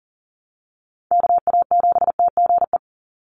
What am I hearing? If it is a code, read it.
Morse code, heard as a keyed tone, about 40 wpm, 700 Hz. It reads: KA7TGE
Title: USA Call Signs 40wpm